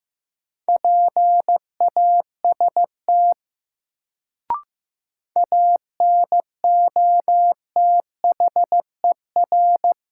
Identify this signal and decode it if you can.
Morse code, heard as a keyed tone, about 16 wpm, 700 Hz. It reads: PAST ANOTHER
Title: Sentences from Top 600 Words - New Word Drill - 15wpm